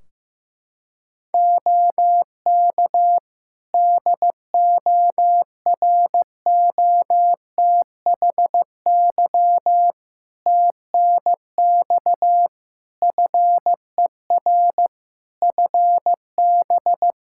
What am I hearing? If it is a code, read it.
Morse code, heard as a keyed tone, about 15 wpm, 700 Hz. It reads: OK DOROTHY TNX FER FB
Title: Practice QSO - Exchange 4 - Ending - 15wpm